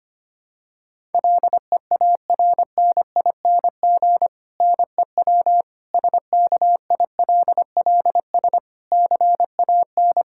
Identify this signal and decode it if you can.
Morse code, heard as a keyed tone, about 25 wpm, 700 Hz. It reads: LEARNING NEW SKILLS CAN
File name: practice / 25wpm / Difficult Sentences with Hints - 25wpm